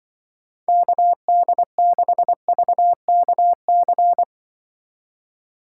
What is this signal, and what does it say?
Morse code, heard as a keyed tone, about 23 wpm, 700 Hz. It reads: KD6VKC
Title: Speed Racing - USA 2x3 Call Signs 20wpm